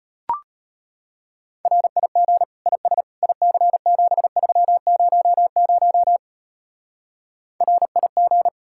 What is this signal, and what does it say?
Morse code, heard as a keyed tone, about 29 wpm, 700 Hz. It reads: RIG IS IC7300 RIG
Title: Speed-Racing Common QSO Phrases Encore - 25wpm